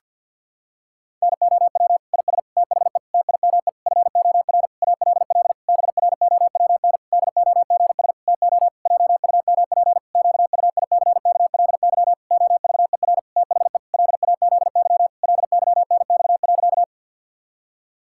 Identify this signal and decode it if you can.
Morse code, heard as a keyed tone, about 51 wpm, 700 Hz. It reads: NOW IS THE TIME FOR ALL BROWN DOGS TO JUMP QUICKLY OVER THE LAZY LYNX.
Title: Pangram Challenge v2 - 50wpm